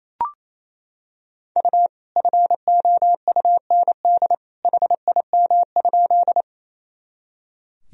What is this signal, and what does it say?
Morse code, heard as a keyed tone, about 28 wpm, 700 Hz. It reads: U FOUND HIM?
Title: Sentences from Top 300 Words and Abbreviations 28wpm